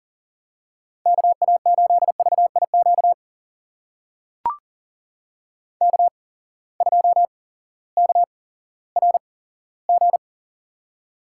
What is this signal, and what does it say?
Morse code, heard as a keyed tone, about 40 wpm, spaced as for 18 wpm, 690 Hz. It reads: KA8VIQ K2KRG
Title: USA Call Signs - 8x Character Spacing - 40wpm